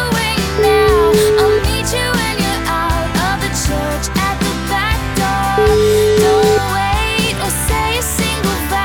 Ringback tone (Europe, 425 Hz), with noise.